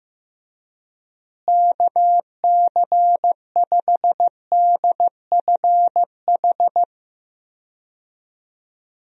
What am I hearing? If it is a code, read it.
Morse code, heard as a keyed tone, about 15 wpm, 700 Hz. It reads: KC5DFH
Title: USA Call Signs - 6 Characters - Repeated 3x - 15wpm